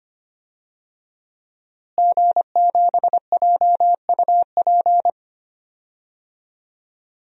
Morse code, heard as a keyed tone, about 25 wpm, 700 Hz. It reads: G7JUP
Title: DXCC-100 Call Signs - 5 Characters - Repeated 2x - 25wpm